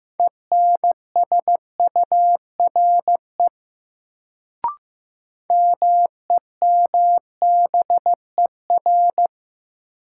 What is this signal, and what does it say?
Morse code, heard as a keyed tone, about 15 wpm, 700 Hz. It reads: ENSURE MEMBER
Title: Sentences from Top 1100 Words - New Word Drill - 15wpm